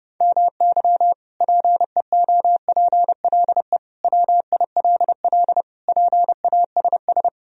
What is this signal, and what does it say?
Morse code, heard as a keyed tone, about 30 wpm, 700 Hz. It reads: MY PEOPLE WILL PASS
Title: Sentences from Top 1000 Words - Review Part 2 - 30wpm